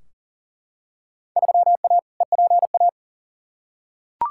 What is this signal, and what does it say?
Morse code, heard as a keyed tone, about 40 wpm, 700 Hz. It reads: 3A EPA
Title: Field Day Exchanges - 40wpm